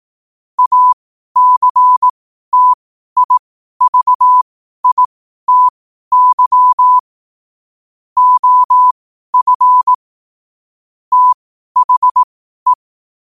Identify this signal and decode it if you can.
Morse code, heard as a keyed tone, about 18 wpm, spaced as for 12 wpm, 1000 Hz. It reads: ACTIVITY OF THE
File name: qotdfiles / 12wpm